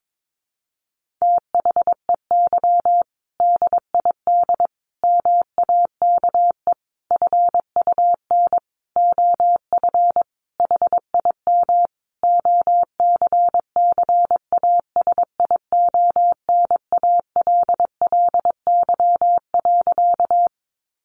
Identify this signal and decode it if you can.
Morse code, heard as a keyed tone, about 22 wpm, 700 Hz. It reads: THEY DID MAKE FUN OF HIM OCCASIONALLY.
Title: Sentences from Top 1300 Words 22wpm